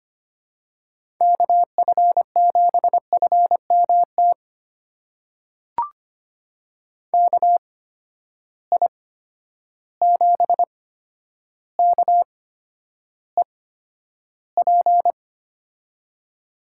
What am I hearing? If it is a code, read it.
Morse code, heard as a keyed tone, about 25 wpm, spaced as for 11 wpm, 700 Hz. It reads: KF7FMT KI7KEP
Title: USA Call Signs - 6 Characters - 8x Character Spacing - 25wpm